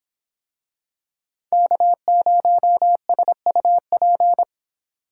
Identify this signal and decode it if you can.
Morse code, heard as a keyed tone, about 26 wpm, 700 Hz. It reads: K0SUP